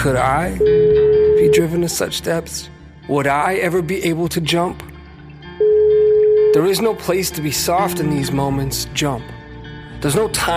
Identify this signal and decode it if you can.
Ringback tone (Europe, 425 Hz), with speech.